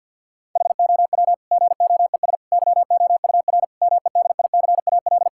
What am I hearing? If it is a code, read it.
Morse code, heard as a keyed tone, about 50 wpm, 700 Hz. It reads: HOW GOES YOUR MEDICAL
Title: Sentences from Top 1000 Words - Review Part 5 - 50wpm